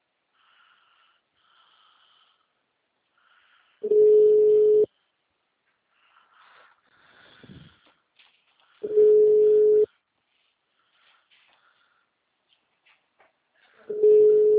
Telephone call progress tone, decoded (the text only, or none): ringback tone